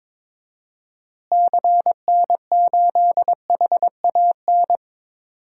Morse code, heard as a keyed tone, about 21 wpm, 700 Hz. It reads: CN8HAN
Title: DXCC-100 Call Signs - 6 Characters - Repeated 2x - 22wpm